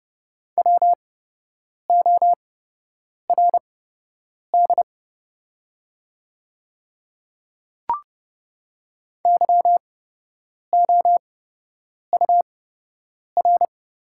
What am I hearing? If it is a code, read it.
Morse code, heard as a keyed tone, about 30 wpm, spaced as for 8 wpm, 700 Hz. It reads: WORD YOUR
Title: 8x Character Spacing - Top 100 Words - 30wpm